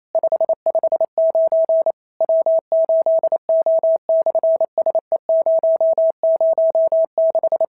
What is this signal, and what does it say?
Morse code, heard as a keyed tone, about 28 wpm, 650 Hz. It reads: 559 W8O/SE006
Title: SOTA QSO Practice 28wpm